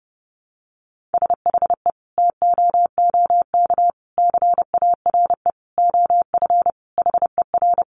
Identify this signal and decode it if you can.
Morse code, heard as a keyed tone, about 30 wpm, 700 Hz. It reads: SHE TOOK CARE OF HER
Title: Sentences from Top 1300 Words 30wpm